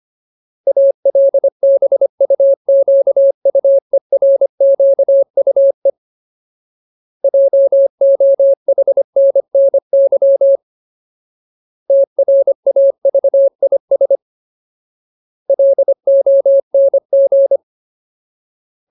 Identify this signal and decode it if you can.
Morse code, heard as a keyed tone, about 25 wpm, 550 Hz. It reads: ALBUQUERQUE JOHNNY TRAVIS LONG
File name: Ultimate Word List / 25WPM Ultimate